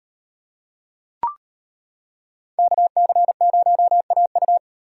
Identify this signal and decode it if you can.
Morse code, heard as a keyed tone, about 38 wpm, 700 Hz. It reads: KC0AU